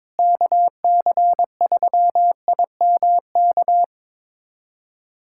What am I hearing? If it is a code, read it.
Morse code, heard as a keyed tone, about 22 wpm, 700 Hz. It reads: KC3IMK
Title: USA Callsign DE Callsign - 6 Characters - 22wpm